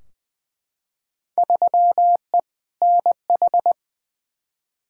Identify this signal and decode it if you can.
Morse code, heard as a keyed tone, about 20 wpm, 700 Hz. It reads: 3E NH
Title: Field Day Exchanges Encore - 20wpm